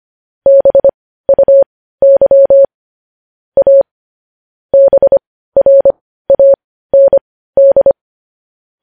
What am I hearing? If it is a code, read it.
Morse code, heard as a keyed tone, about 25 wpm, spaced as for 15 wpm, 550 Hz. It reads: BUY A BRAND
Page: news headlines in morse code at 15 wpm